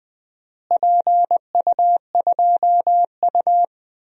Morse code, heard as a keyed tone, about 20 wpm, 700 Hz. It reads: PU2U
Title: DXCC-100 Call Signs - 4 Characters - Repeated 3x - 20wpm